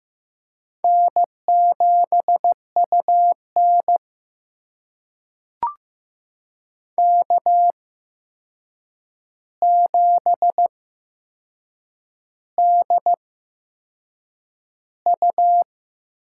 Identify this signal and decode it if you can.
Morse code, heard as a keyed tone, about 15 wpm, spaced as for 6 wpm, 700 Hz. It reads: N7UN K7DU
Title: USA Call Signs - 4 Characters - 8x Character Spacing - 15wpm